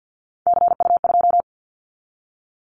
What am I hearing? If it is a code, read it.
Morse code, heard as a keyed tone, about 50 wpm, 690 Hz. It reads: CU2
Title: Callsign Prefixes for Top 100 DXCC - 50wpm